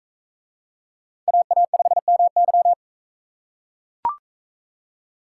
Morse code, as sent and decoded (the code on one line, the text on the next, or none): .- .- ..... -- -.--
AA5MY